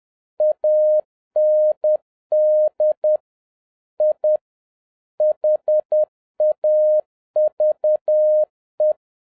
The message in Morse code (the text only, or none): AND I HAVE